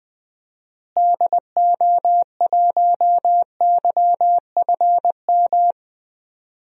Morse code, heard as a keyed tone, about 21 wpm, 700 Hz. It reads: DO1YFM
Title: DXCC-100 Call Signs - 6 Characters - Repeated 2x - 20wpm